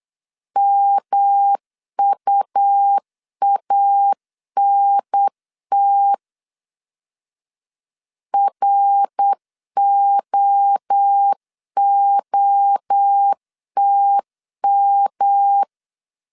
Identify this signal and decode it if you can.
Morse code, heard as a keyed tone, about 8 wpm, 800 Hz. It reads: MUANT ROOTM
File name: CW / DET / 1 3 1500C S B 8wpm 4141